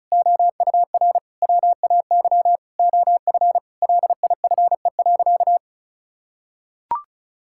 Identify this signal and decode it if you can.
Morse code, heard as a keyed tone, about 34 wpm, 700 Hz. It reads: OUR WAY OF LIFE.
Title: Sentences from Top 1000 Words - Review Part 6 - 35wpm